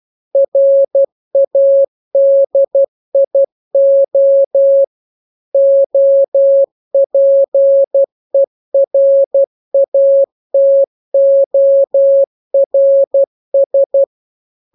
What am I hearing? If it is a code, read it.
Morse code, heard as a keyed tone, about 12 wpm, 550 Hz. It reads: RADIO OPERATORS
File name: jokes / joke 2 12 0000